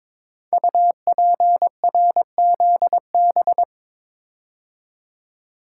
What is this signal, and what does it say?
Morse code, heard as a keyed tone, about 22 wpm, 700 Hz. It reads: UPRZB
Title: Code Groups - 5 Letters - 22wpm